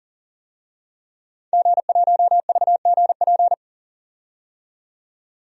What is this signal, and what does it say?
Morse code, heard as a keyed tone, about 40 wpm, 700 Hz. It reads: G1VGP
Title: DXCC-100 Call Signs - 5 Characters - 40wpm